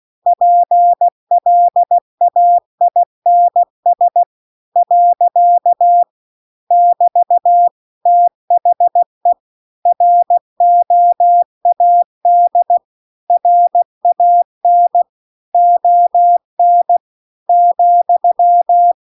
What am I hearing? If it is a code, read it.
Morse code, heard as a keyed tone, about 16 wpm, 700 Hz. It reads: PLAINS . <BT> THE ROAD RAN ON ,